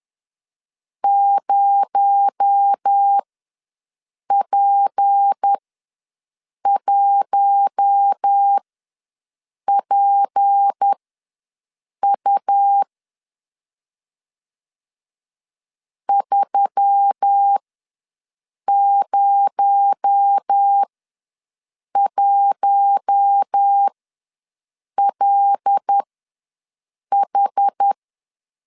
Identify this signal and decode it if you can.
Morse code, heard as a keyed tone, about 10 wpm, spaced as for 6 wpm, 800 Hz. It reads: 0P1PU 301LH
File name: LEZIONI / 6 PKX0 / 6 PKX0 ws 10wpm 4220